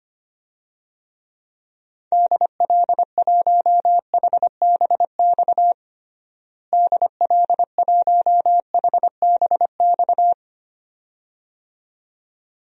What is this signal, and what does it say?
Morse code, heard as a keyed tone, about 25 wpm, 700 Hz. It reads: DL1HBX DL1HBX
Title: DXCC-100 Call Signs - 6 Characters - Repeated 2x - 25wpm